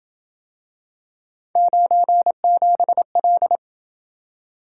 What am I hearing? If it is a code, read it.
Morse code, heard as a keyed tone, about 28 wpm, 700 Hz. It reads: 97L